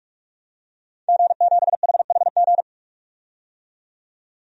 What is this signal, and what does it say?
Morse code, heard as a keyed tone, about 46 wpm, 700 Hz. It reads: G7HHG
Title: DXCC-100 Call Signs - 5 Characters - 45wpm